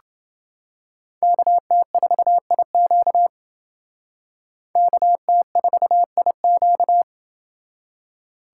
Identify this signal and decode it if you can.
Morse code, heard as a keyed tone, about 29 wpm, 700 Hz. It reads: KT4IQ KT4IQ